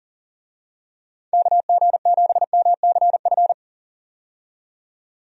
Morse code, heard as a keyed tone, about 37 wpm, 700 Hz. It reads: KG7MCF
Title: USA Call Signs - 6 Characters - 40wpm